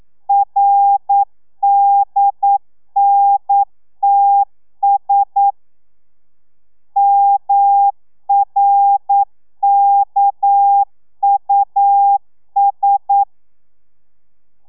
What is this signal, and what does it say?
Morse code, heard as a keyed tone, about 9 wpm, 800 Hz. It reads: RDNTS MRKUS
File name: telegrafia / i2tfj / Lez6 40